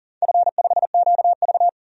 Morse code, heard as a keyed tone, about 40 wpm, 700 Hz. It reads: F5QV
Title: DXCC-100 Call Signs - 4 Characters - 40wpm